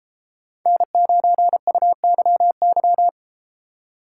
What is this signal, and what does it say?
Morse code, heard as a keyed tone, about 31 wpm, 700 Hz. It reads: N9UYY